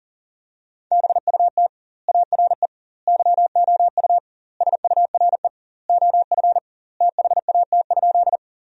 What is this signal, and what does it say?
Morse code, heard as a keyed tone, about 40 wpm, 700 Hz. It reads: BUT ARE YOU SURE OF THAT?
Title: 2x Word-Spacing - Sentences from Top 300 Words - 40wpm